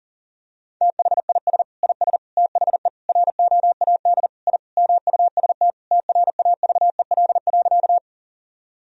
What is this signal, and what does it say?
Morse code, heard as a keyed tone, about 40 wpm, 700 Hz. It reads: THIS IS THE ROAD I MUST TRAVEL.
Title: Sentences from Top 1000 Words - Review Part 2 - 40wpm